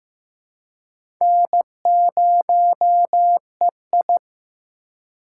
Morse code, heard as a keyed tone, about 15 wpm, 700 Hz. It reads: N0EI
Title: USA Call Signs - 4 Characters - 3x Character Spacing - 15wpm